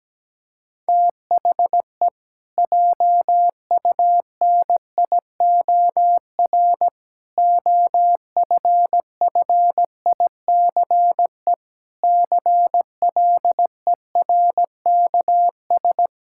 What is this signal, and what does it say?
Morse code, heard as a keyed tone, about 17 wpm, 700 Hz. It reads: THE JUNIOR OFFICE CLERKS
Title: Pangram Challenge v2 - 17wpm